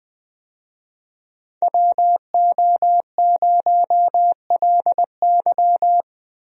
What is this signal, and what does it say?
Morse code, heard as a keyed tone, about 20 wpm, 700 Hz. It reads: WO0LY
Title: USA Call Signs - 5 Characters - Repeated 3x - 20wpm